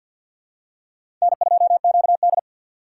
Morse code, heard as a keyed tone, about 53 wpm, 700 Hz. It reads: N2QD
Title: USA Call Signs - 4 Characters - 6x Character Spacing - 50wpm